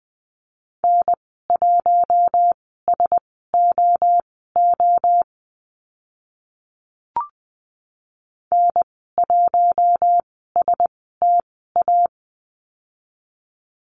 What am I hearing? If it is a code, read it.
Morse code, heard as a keyed tone, about 20 wpm, spaced as for 14 wpm, 700 Hz. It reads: N1SOO N1STA
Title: 042 Lesson 8g - Number 1 - Call Signs 20wpm